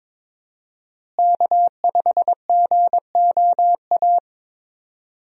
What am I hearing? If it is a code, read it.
Morse code, heard as a keyed tone, about 22 wpm, 700 Hz. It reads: K5GOA
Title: USA Call Signs - 5 Characters - Repeated 3x - 22wpm